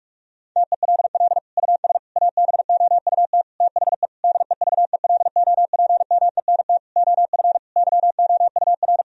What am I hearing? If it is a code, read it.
Morse code, heard as a keyed tone, about 46 wpm, 700 Hz. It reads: TELL US ABOUT THE DEVELOPMENT OF YOUR
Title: Sentences from Top 1000 Words - Review Part 4 - 45wpm